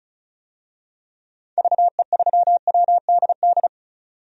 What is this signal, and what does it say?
Morse code, heard as a keyed tone, about 34 wpm, 700 Hz. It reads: VE3WDD
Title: DXCC-100 Call Signs - 6 Characters - 35wpm